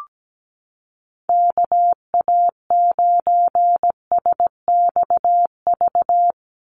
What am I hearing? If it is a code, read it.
Morse code, heard as a keyed tone, about 17 wpm, 700 Hz. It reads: KA9SXV
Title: Call Signs - 17wpm